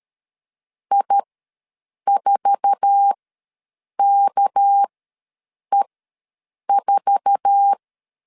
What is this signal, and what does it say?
Morse code, heard as a keyed tone, about 13 wpm, spaced as for 7 wpm, 800 Hz. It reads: I4KE4
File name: LEZIONI / 8 YJ45 / 8 YJ4 ws 12wpm 3249